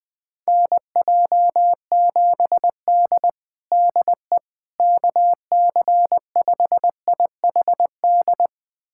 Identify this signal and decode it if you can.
Morse code, heard as a keyed tone, about 21 wpm, 700 Hz. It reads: NJ7D DE KC5IHD
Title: USA Callsign DE Callsign 20wpm